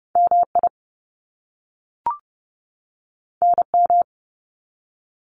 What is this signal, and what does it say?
Morse code, heard as a keyed tone, about 30 wpm, 700 Hz. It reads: MI NM
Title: US States 30wpm